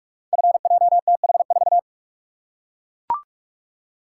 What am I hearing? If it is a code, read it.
Morse code, heard as a keyed tone, about 44 wpm, 690 Hz. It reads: FJTH4
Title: Code Groups - 5 Letter:Numbers - 45wpm